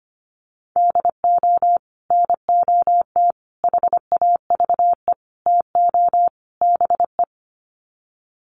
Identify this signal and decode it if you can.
Morse code, heard as a keyed tone, about 25 wpm, 700 Hz. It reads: DO NOT HAVE TO BE
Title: Sets of 5 Words 25wpm